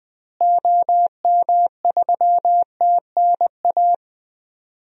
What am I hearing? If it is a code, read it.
Morse code, heard as a keyed tone, about 20 wpm, 700 Hz. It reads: OM3TNA
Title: DXCC-100 Call Signs - 6 Characters - Repeated 3x - 20wpm